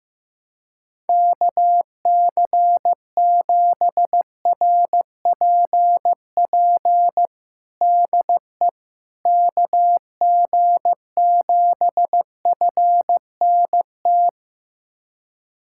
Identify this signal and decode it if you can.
Morse code, heard as a keyed tone, about 15 wpm, 700 Hz. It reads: KC7RPP DE KG7FNT
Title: USA Callsign DE Callsign - 6 Characters - 15wpm